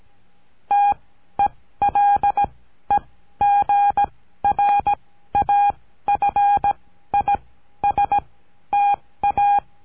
Morse code, heard as a keyed tone, about 17 wpm, spaced as for 12 wpm, 810 Hz. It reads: TELEGRAFISTA
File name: CURSO DE TELEGRAFIA